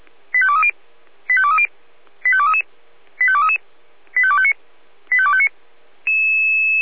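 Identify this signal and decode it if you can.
Five-tone selective call (ZVEI), heard as quick runs of five tones, tones of about 70 ms. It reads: 85230 85230 85233 85233 85259 85259
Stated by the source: ZVEI five-tone selective call 85252, received by SDR.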